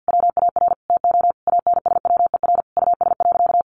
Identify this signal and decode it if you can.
Morse code, heard as a keyed tone, about 50 wpm, 700 Hz. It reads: WAR TO ANSWER US.
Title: Sentences from Top 1500 Words 50wpm